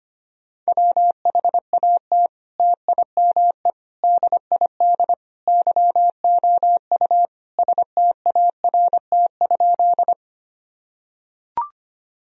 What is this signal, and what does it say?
Morse code, heard as a keyed tone, about 25 wpm, 700 Hz. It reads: WHAT TIME DID YOU START?
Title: Sentences from Top 1000 Words - Review Part 1 - 25wpm